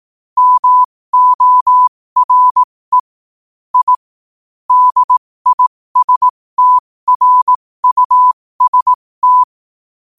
Morse code, heard as a keyed tone, about 18 wpm, spaced as for 15 wpm, 1000 Hz. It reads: MORE I DISTRUST
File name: qotdfiles / 15wpm